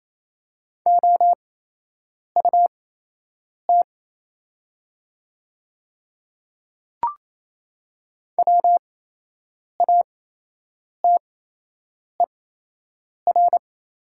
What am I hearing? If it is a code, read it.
Morse code, heard as a keyed tone, about 28 wpm, spaced as for 8 wpm, 690 Hz. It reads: OUT WATER
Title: 8x Character Spacing - Top 100 Words - 28wpm